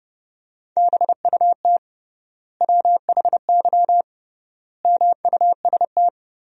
Morse code, heard as a keyed tone, about 32 wpm, 700 Hz. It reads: BUT WHY MUST
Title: 3x Word-Spacing - Sentences from Top 400 Words - 30wpm